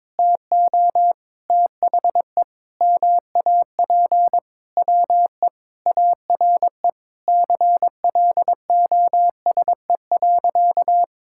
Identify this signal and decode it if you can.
Morse code, heard as a keyed tone, about 22 wpm, 700 Hz. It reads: TO THE MAP WE ARE CLOSE.
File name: practice / 22wpm / Sentences with 2-Word Phrases - 22wpm